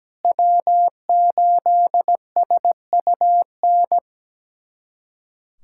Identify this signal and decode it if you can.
Morse code, heard as a keyed tone, about 17 wpm, 700 Hz. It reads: W8SUN
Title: USA Callsign DE Callsign - 5 Characters - 17wpm